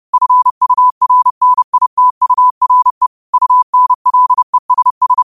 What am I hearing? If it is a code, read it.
Morse code, heard as a keyed tone, about 30 wpm, 1000 Hz. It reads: FURNITURE UNLESS